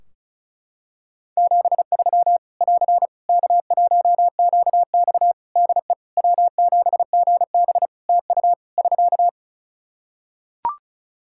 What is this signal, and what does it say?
Morse code, heard as a keyed tone, about 34 wpm, 700 Hz. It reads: 73 <AR> K1QX DE W7GB TU <SK>
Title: K1USN SST QSO Practice 35wpm